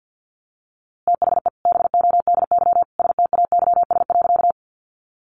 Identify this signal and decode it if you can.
Morse code, heard as a keyed tone, about 50 wpm, 700 Hz. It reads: THE BODY STAYS.
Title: A Sound Of Thunder 50wpm